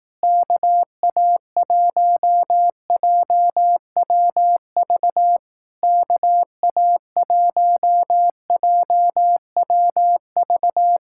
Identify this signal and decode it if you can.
Morse code, heard as a keyed tone, about 18 wpm, 700 Hz. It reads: KA1JWV KA1JWV
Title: Speed-Racing Common QSO Phrases Encore - 15wpm